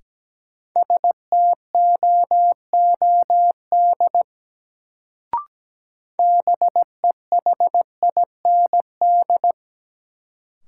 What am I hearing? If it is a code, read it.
Morse code, heard as a keyed tone, about 17 wpm, 700 Hz. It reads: STOOD BEHIND
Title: Sentences from Top 500 Words - New Word Drill - 17wpm